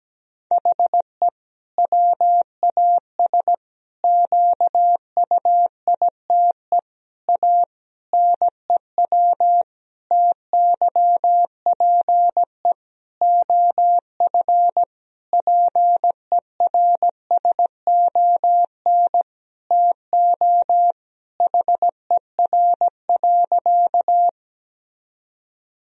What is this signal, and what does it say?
Morse code, heard as a keyed tone, about 17 wpm, 700 Hz. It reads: HE WAS QUITE A NEW TYPE OF PERSON TO HER.
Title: Sentences from Top 1000 Words - Review Part 3 - 17wpm